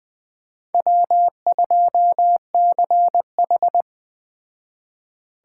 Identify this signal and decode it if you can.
Morse code, heard as a keyed tone, about 20 wpm, 700 Hz. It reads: W2CH